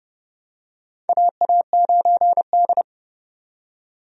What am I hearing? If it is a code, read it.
Morse code, heard as a keyed tone, about 30 wpm, 700 Hz. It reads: AA9D
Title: USA Call Signs - 4 Characters - 30wpm